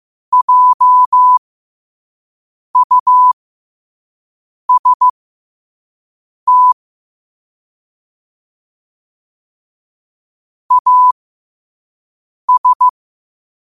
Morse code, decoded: JUST AS

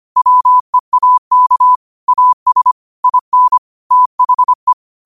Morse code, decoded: WEAK AS IN THE

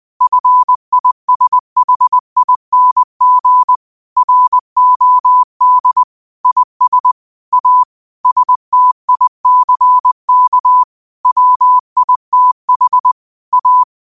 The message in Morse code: FISHING ROD IS A STICK WITH A

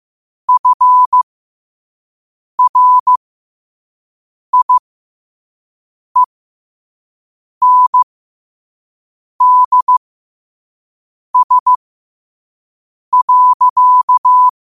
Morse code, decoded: FRIENDS.

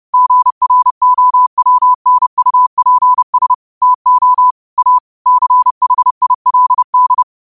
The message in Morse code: GROWNUPS TO A CHILD

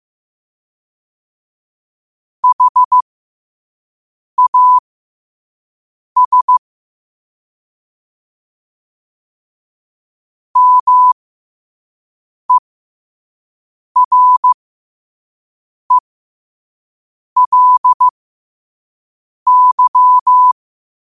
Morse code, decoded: HAS MERELY

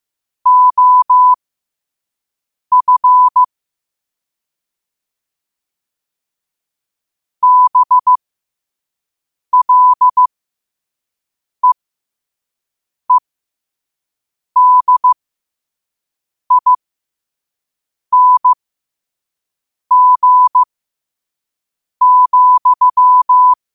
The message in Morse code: OF BLEEDING,

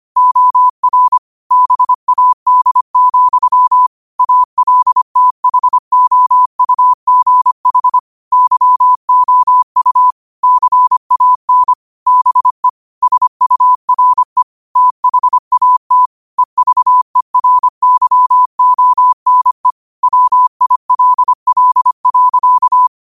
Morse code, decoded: OR BAD, ALTHOUGH YOU CAN BE SURE THAT EVERYONE WILL.

